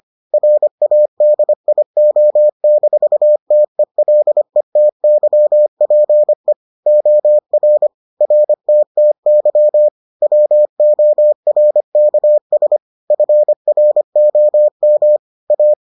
Morse code, heard as a keyed tone, about 25 wpm, 600 Hz. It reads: RADIO-TELETYPE OR RTTY WORKS FROM A